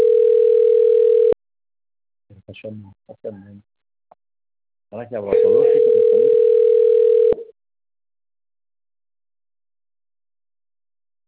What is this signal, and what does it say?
Ringback tone (North America), 440 + 480 Hz, with speech.
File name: rg-1600-unknown- - - .48355